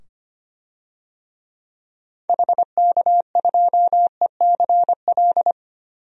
Morse code, heard as a keyed tone, about 25 wpm, 700 Hz. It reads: HK2ECL
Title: DXCC-100 Call Signs - 6 Characters - 25wpm